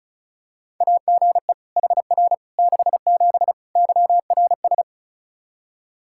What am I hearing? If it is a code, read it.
Morse code, heard as a keyed tone, about 34 wpm, 700 Hz. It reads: AGE HR 67 YRS
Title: K1USN SST QSO Practice 35wpm